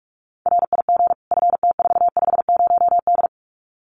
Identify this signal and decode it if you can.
Morse code, heard as a keyed tone, about 44 wpm, 700 Hz. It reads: RIG FT450D
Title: Common QSO Phrases 45wpm